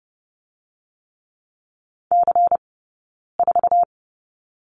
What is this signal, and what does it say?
Morse code, heard as a keyed tone, about 30 wpm, spaced as for 9 wpm, 700 Hz. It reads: C4
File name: Lesson 20d - Number 3 - Two Characters 30wpm